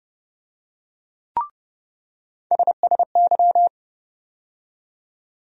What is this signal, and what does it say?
Morse code, heard as a keyed tone, about 30 wpm, 700 Hz. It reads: SSY
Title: USA Call Sign Postfixes - 3 Letters 30wpm